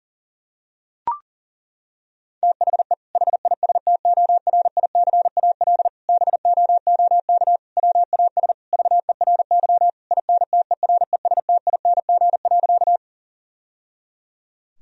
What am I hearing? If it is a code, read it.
Morse code, heard as a keyed tone, about 40 wpm, 690 Hz. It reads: THE HISTORICAL BOOK WAS VERY INTERESTING.
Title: Sentences from Top 1000 Words - Review Part 4 - 40wpm